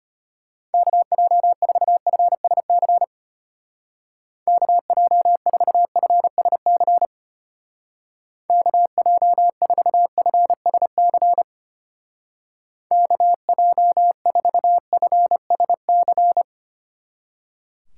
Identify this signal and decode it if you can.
Morse code, heard as a keyed tone, about 29 wpm, 700 Hz. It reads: KJ4FSC KJ4FSC KJ4FSC KJ4FSC
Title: Speed Racing - USA 2x3 Call Signs 25wpm